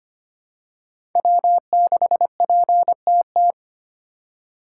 Morse code, heard as a keyed tone, about 25 wpm, 700 Hz. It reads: W6PTT
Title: USA Call Signs - 5 Characters - 3x Character Spacing - 25wpm